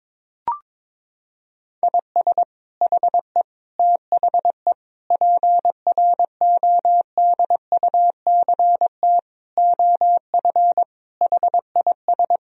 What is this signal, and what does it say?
Morse code, heard as a keyed tone, about 22 wpm, 700 Hz. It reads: IS HE THE PRODUCT OF HIS
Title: Sentences from Top 1000 Words - Review Part 2 - 22wpm